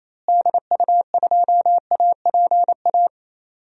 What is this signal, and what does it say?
Morse code, heard as a keyed tone, about 28 wpm, 700 Hz. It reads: DU2APA
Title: DXCC-100 Call Signs - 6 Characters - Repeated 3x - 28wpm